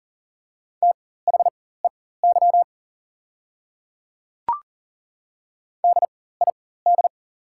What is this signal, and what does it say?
Morse code, heard as a keyed tone, about 37 wpm, spaced as for 18 wpm, 690 Hz. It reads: THEY DID
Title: 4x Character Spacing - Top 100 Words - 40wpm